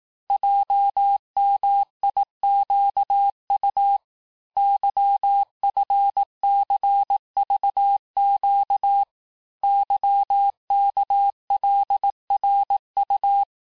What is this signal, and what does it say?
Morse code, heard as a keyed tone, about 18 wpm, 800 Hz. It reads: JMIQU YFCVQ YKLRU
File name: cours-du-lundi files / cours-1588-texte-chiffre-vitesse-17MPM